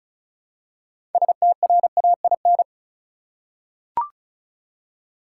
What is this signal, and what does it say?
Morse code, heard as a keyed tone, about 34 wpm, 700 Hz. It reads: STRAIN
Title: Six Letter Words - 35wpm